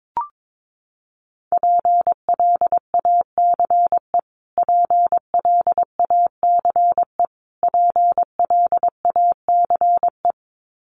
Morse code, heard as a keyed tone, about 22 wpm, 700 Hz. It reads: PLACE PLACE PLACE
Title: Top 500 Words Repeated 22wpm